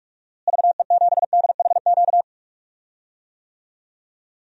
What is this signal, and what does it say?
Morse code, heard as a keyed tone, about 43 wpm, 700 Hz. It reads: VE7DHQ
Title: DXCC-100 Call Signs - 6 Characters - Repeated 3x - 45wpm